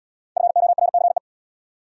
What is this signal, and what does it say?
Morse code, heard as a keyed tone, about 80 wpm, 700 Hz. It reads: FORCE